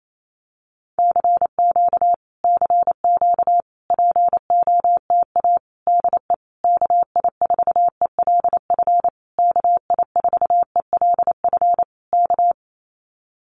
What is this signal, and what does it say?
Morse code, heard as a keyed tone, about 28 wpm, 700 Hz. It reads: CQ CQ POTA DE KI4ELF KI4ELF K
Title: Common QSO Phrases Encore 28wpm